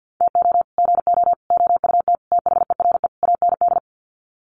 Morse code, heard as a keyed tone, about 50 wpm, 700 Hz. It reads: TO GO OUT THERE AND
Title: Sets of 5 Words Encore2 50wpm